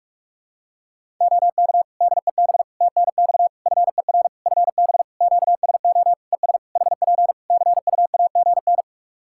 Morse code, heard as a keyed tone, about 47 wpm, 700 Hz. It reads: OK DEB TNX FER FB QSO ES HP CUAGN
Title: Practice QSO - Exchange 4 - Ending - 45wpm